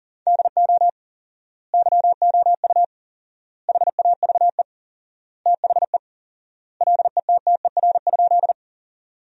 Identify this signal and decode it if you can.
Morse code, heard as a keyed tone, about 40 wpm, 700 Hz. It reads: DO YOU HAVE THE LETTER?
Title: 4x Word-Spacing - Sentences from Top 300 Words - 40wpm